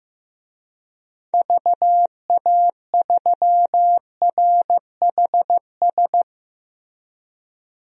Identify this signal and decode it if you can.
Morse code, heard as a keyed tone, about 15 wpm, 700 Hz. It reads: VA3RHS
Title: DXCC-100 Call Signs - 6 Characters - Repeated 2x - 15wpm